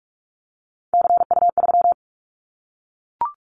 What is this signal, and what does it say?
Morse code, heard as a keyed tone, about 46 wpm, 700 Hz. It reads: CU3
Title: Callsign Prefixes for Top 50 DXCC - 45wpm